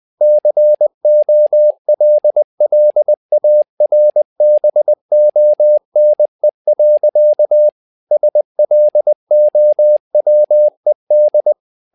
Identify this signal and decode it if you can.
Morse code, heard as a keyed tone, about 20 wpm, 600 Hz. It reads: COLLARBONE. SLOWED
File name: tower trouble 20wpm